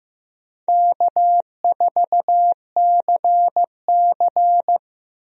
Morse code, heard as a keyed tone, about 15 wpm, 700 Hz. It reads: K4CC
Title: USA Call Signs - 4 Characters - Repeated 3x - 15wpm